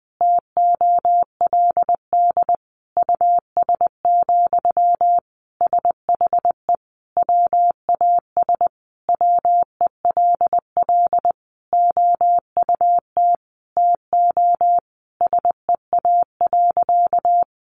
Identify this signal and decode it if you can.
Morse code, heard as a keyed tone, about 20 wpm, 700 Hz. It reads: TOLD US, SHE WAS WELL OUT TO SEA.
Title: Sentences from Top 400 Words 20wpm